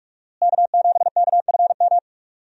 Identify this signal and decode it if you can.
Morse code, heard as a keyed tone, about 44 wpm, 700 Hz. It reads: K7KFM
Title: USA Call Signs - 5 Characters - Repeated 2x - 45wpm